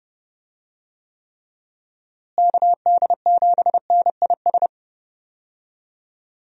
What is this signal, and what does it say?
Morse code, heard as a keyed tone, about 30 wpm, 700 Hz. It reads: KD7NIS
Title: USA Call Signs - 6 Characters - Repeated 2x - 30wpm